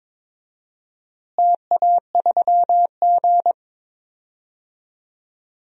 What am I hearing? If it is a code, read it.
Morse code, heard as a keyed tone, about 22 wpm, 700 Hz. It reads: TA3G
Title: DXCC-100 Call Signs - 4 Characters - Repeated 3x - 22wpm